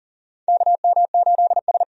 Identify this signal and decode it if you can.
Morse code, heard as a keyed tone, about 40 wpm, 700 Hz. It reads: KM8S
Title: USA Call Signs - 4 Characters - 40wpm